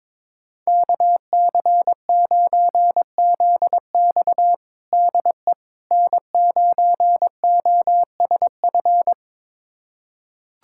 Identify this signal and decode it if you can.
Morse code, heard as a keyed tone, about 22 wpm, 700 Hz. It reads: KC9ZX DE N9OSF
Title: USA Callsign DE Callsign - 5 Characters - 22wpm